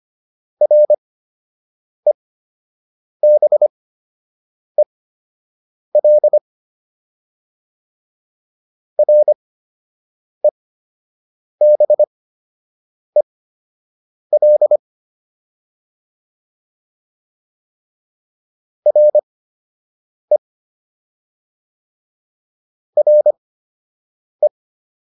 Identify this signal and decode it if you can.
Morse code, heard as a keyed tone, about 25 wpm, spaced as for 7 wpm, 600 Hz. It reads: REBEL REBEL RE RE